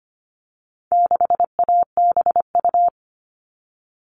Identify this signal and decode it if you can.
Morse code, heard as a keyed tone, about 25 wpm, 700 Hz. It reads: 6ABU
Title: All Cut Numbers - 4-Digit 25wpm